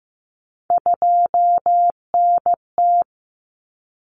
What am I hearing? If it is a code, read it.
Morse code, heard as a keyed tone, about 15 wpm, 700 Hz. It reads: 2NT
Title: Common Cut Numbers - 3-Digit 15wpm